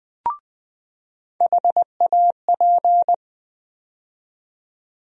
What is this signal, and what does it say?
Morse code, heard as a keyed tone, about 20 wpm, 700 Hz. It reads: HAP
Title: USA Call Sign Postfixes - 3 Letters 20wpm